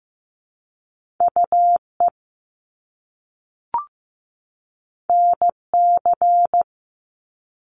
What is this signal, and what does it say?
Morse code, heard as a keyed tone, about 15 wpm, 700 Hz. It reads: UE NC
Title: Top Two Letter Combinations in Words 15wpm